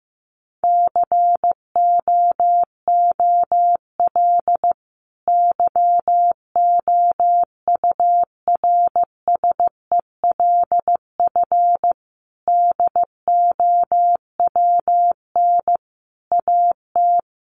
Morse code, heard as a keyed tone, about 15 wpm, 700 Hz. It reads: COOL YOURSELF DOWN AT
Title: Sentences from Top 1200 Words 15wpm